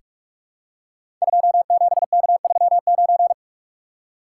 Morse code, heard as a keyed tone, about 44 wpm, 700 Hz. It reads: 27K39
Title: Code Groups - 5 Letter:Numbers - 45wpm